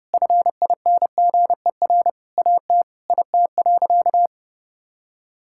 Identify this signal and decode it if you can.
Morse code, heard as a keyed tone, about 30 wpm, 700 Hz. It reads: FINGER AT IT.
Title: Sentences from Top 1600 Words 30wpm